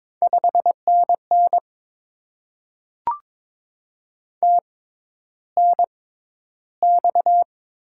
Morse code, heard as a keyed tone, about 22 wpm, spaced as for 11 wpm, 700 Hz. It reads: 5NN TNX
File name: Top 20 QSO Elements - 6x Character Spacing - 22wpm